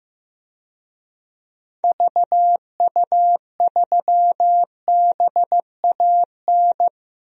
Morse code, heard as a keyed tone, about 15 wpm, 700 Hz. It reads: VU3BAN
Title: DXCC-100 Call Signs - 6 Characters - 15wpm